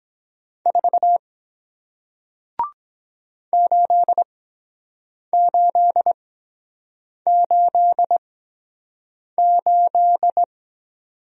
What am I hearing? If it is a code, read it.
Morse code, heard as a keyed tone, about 23 wpm, spaced as for 7 wpm, 700 Hz. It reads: 4 8888